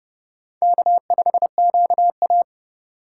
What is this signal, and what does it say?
Morse code, heard as a keyed tone, about 30 wpm, 700 Hz. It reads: K5QA
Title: USA Call Signs - 2x Character Spacing - 30wpm